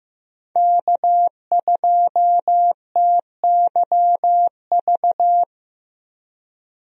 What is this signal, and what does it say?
Morse code, heard as a keyed tone, about 15 wpm, 700 Hz. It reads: K2TYV
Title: USA Call Signs - 5 Characters - Repeated 2x - 15wpm